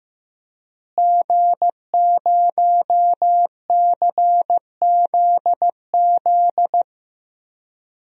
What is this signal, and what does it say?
Morse code, heard as a keyed tone, about 15 wpm, 700 Hz. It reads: G0CZZ
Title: DXCC-100 Call Signs - 5 Characters - Repeated 3x - 15wpm